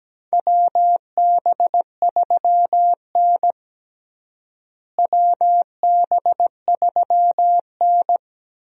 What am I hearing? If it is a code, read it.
Morse code, heard as a keyed tone, about 17 wpm, 700 Hz. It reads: WB3N WB3N
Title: USA Call Signs - 4 Characters - Repeated 2x - 17wpm